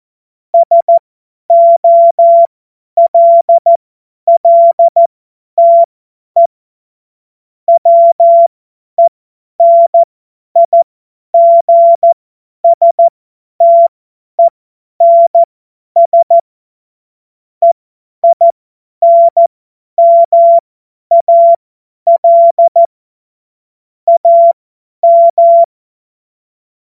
Morse code, heard as a keyed tone, about 14 wpm, spaced as for 10 wpm, 680 Hz. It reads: SOLLTE WENIGSTENS EINMAL AM
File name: Morsekurs / Witze / Witz 05. Oktober 2021, 50 F70